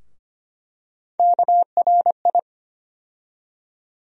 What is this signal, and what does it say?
Morse code, heard as a keyed tone, about 25 wpm, 700 Hz. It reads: KRI
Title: USA Call Sign Postfixes - 3 Letters 25wpm